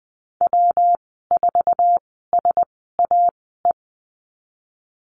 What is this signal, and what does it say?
Morse code, heard as a keyed tone, about 20 wpm, spaced as for 15 wpm, 700 Hz. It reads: W4SAE
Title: 161 Lesson 26g - Letter g - Call Signs 20wpm